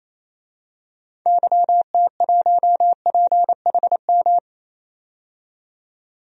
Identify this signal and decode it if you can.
Morse code, heard as a keyed tone, about 28 wpm, 700 Hz. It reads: YT1PHM
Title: DXCC-100 Call Signs - 6 Characters - 28wpm